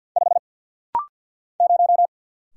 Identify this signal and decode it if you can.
Morse code, heard as a keyed tone, about 50 wpm, 700 Hz. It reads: 5 0
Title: Single Letter-Number - Rapid-Fire 50wpm